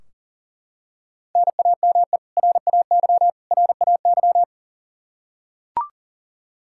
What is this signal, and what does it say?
Morse code, heard as a keyed tone, about 40 wpm, 700 Hz. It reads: NAME RAY RAY
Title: Practice QSO - Exchange 1 - Beginning - 40wpm